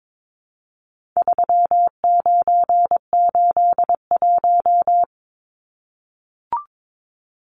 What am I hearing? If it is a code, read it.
Morse code, heard as a keyed tone, about 22 wpm, 700 Hz. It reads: 3981
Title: Numbers - 4-Digit 22wpm